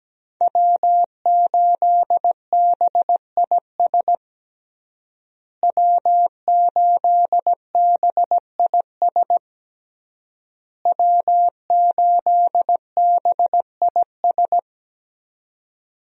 Morse code, heard as a keyed tone, about 17 wpm, 700 Hz. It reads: W8BIS W8BIS W8BIS
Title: USA Call Signs - 5 Characters - Repeated 3x - 17wpm